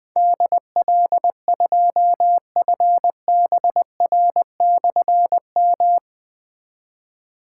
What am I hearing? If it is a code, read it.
Morse code, heard as a keyed tone, about 20 wpm, 700 Hz. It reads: DL2FBR/M
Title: Difficult Call Signs 20wpm